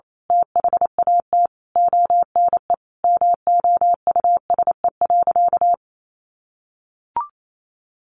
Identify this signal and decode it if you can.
Morse code, heard as a keyed tone, about 28 wpm, 700 Hz. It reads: THAT ONE MOUSE.
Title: A Sound Of Thunder 28wpm